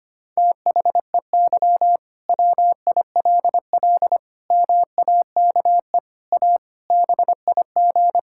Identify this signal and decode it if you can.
Morse code, heard as a keyed tone, about 25 wpm, 700 Hz. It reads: THEY WILL MAKE A BIG
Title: Sentences from Top 1000 Words - Review Part 5 - 25wpm